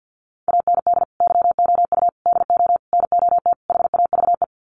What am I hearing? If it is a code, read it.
Morse code, heard as a keyed tone, about 50 wpm, 700 Hz. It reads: AND YOU DO NOT HAVE
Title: Sets of 5 Words Encore 50wpm